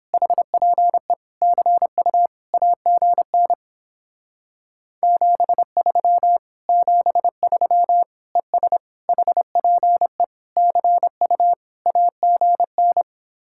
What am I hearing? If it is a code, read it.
Morse code, heard as a keyed tone, about 29 wpm, 700 Hz. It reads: HPE CU AGN 73 73 ES HPE CU AGN